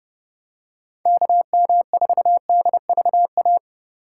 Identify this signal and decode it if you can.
Morse code, heard as a keyed tone, about 30 wpm, 700 Hz. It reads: KM4DVA
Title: USA Call Signs - 6 Characters - 3x Character Spacing - 30wpm